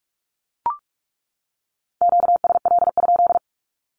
Morse code, heard as a keyed tone, about 44 wpm, 700 Hz. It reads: QSL?
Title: Q-Codes 45wpm